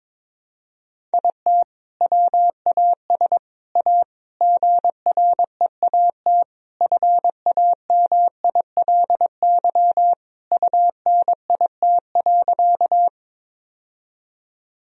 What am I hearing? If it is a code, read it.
Morse code, heard as a keyed tone, about 21 wpm, 700 Hz. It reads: IT WAS A GREAT FAMILY UNIT.
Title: Sentences from Top 1000 Words - Review Part 2 - 22wpm